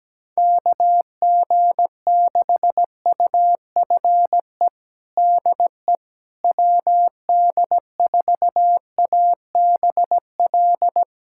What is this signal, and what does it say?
Morse code, heard as a keyed tone, about 17 wpm, 700 Hz. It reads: KG6UFE DE WD4ABL
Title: USA Callsign DE Callsign 17wpm